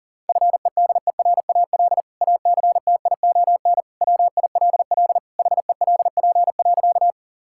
Morse code, heard as a keyed tone, about 40 wpm, 700 Hz. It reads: FEDERAL ACTION WILL HELP.
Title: Sentences from Top 1000 Words - Review Part 5 - 40wpm